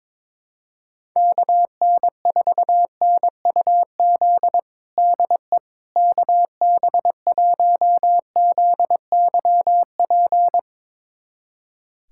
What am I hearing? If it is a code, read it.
Morse code, heard as a keyed tone, about 22 wpm, 700 Hz. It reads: KN4NUZ DE KB1ZYP
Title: USA Callsign DE Callsign 22wpm